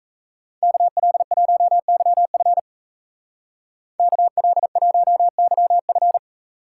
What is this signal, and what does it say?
Morse code, heard as a keyed tone, about 33 wpm, 700 Hz. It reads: KL1YF KL1YF
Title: Speed Racing - USA Call Signs 28wpm